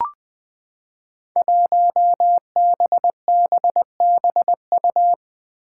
Morse code, heard as a keyed tone, about 20 wpm, 700 Hz. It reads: 1BBBU
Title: Five Challenging Letter-Number Combinations 20wpm